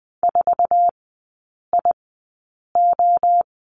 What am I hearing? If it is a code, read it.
Morse code, heard as a keyed tone, about 20 wpm, spaced as for 8 wpm, 700 Hz. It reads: 4IO
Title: 047 Lesson 9e - Number 4 - Three Characters 20wpm